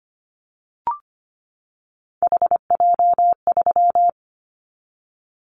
Morse code, heard as a keyed tone, about 25 wpm, 700 Hz. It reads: HJ3
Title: Callsign Prefixes for Top 100 DXCC - 25wpm